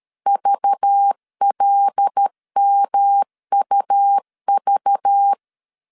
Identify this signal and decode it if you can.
Morse code, heard as a keyed tone, about 13 wpm, 800 Hz. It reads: VLMUV
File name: CW / DET / 1 6 1500C S A 12wpm 3440